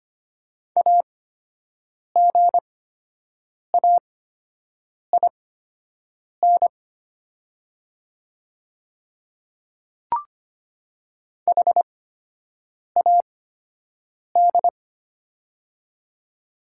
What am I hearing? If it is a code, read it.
Morse code, heard as a keyed tone, about 25 wpm, spaced as for 7 wpm, 700 Hz. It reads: AGAIN HAD
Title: 8x Character Spacing - Top 200 Words - 25wpm